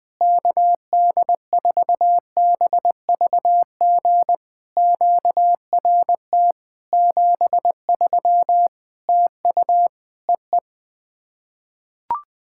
Morse code, heard as a keyed tone, about 20 wpm, 700 Hz. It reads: KD4BVG QRT 73 TU EE
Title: POTA QSO Practice - 20wpm